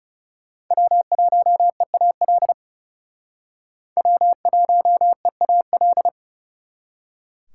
Morse code, heard as a keyed tone, about 36 wpm, 700 Hz. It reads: W1EAL W1EAL
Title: Speed Racing - USA Call Signs 30wpm